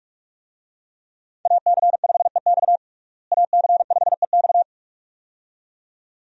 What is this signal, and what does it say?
Morse code, heard as a keyed tone, about 46 wpm, 700 Hz. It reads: AC5EX AC5EX
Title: USA Call Signs - 5 Characters - Repeated 2x - 45wpm